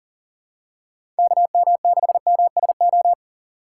Morse code, heard as a keyed tone, about 41 wpm, 700 Hz. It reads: KM6MSO
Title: USA Call Signs - 6 Characters - 4x Character Spacing - 40wpm